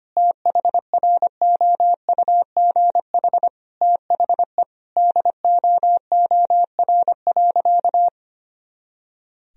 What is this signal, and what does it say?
Morse code, heard as a keyed tone, about 25 wpm, 700 Hz. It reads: THROUGH THE DOOR.
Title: Sentences from Top 400 Words and Abbreviations 25wpm